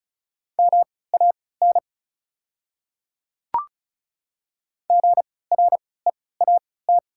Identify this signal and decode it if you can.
Morse code, heard as a keyed tone, about 36 wpm, spaced as for 20 wpm, 700 Hz. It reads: MAN GREAT
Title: 3x Character Spacing - Top 200 Words - 35wpm